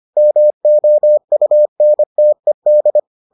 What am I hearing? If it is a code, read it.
Morse code, heard as a keyed tone, about 25 wpm, 600 Hz. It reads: MOUNTED